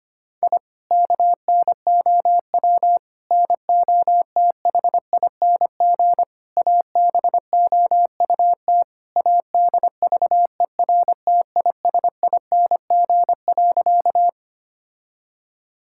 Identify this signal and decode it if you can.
Morse code, heard as a keyed tone, about 25 wpm, 700 Hz. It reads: I KNOW NOTHING ABOUT ADVERTISING.
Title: Sentences from Top 1000 Words - Review Part 6 - 25wpm